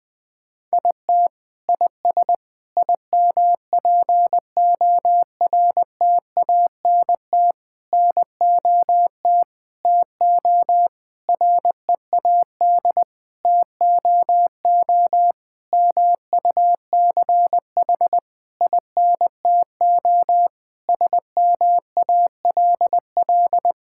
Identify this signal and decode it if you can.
Morse code, heard as a keyed tone, about 21 wpm, 700 Hz. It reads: IT IS IMPORTANT NOT TO READ TOO MUCH INTO SMALL
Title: Sentences with 9-Word Phrases - 20wpm